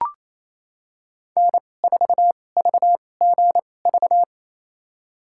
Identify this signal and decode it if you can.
Morse code, heard as a keyed tone, about 28 wpm, spaced as for 20 wpm, 700 Hz. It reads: N4VGV
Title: USA Call Signs - 2x Character Spacing - 28wpm